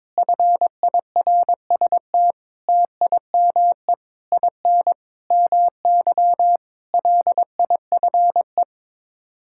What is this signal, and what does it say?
Morse code, heard as a keyed tone, about 22 wpm, 700 Hz. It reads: FIRST TIME IN MY LIFE
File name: practice / 22wpm / Sets of 7 Words - 22wpm